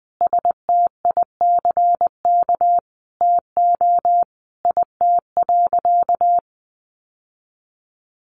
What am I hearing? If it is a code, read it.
Morse code, heard as a keyed tone, about 20 wpm, 700 Hz. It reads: STICK TO IT.
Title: Sentences from Top 1200 Words 20wpm